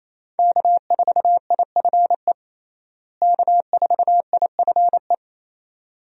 Morse code, heard as a keyed tone, about 28 wpm, 700 Hz. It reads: K4IFE K4IFE
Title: USA Call Signs - 5 Characters - Repeated 2x - 28wpm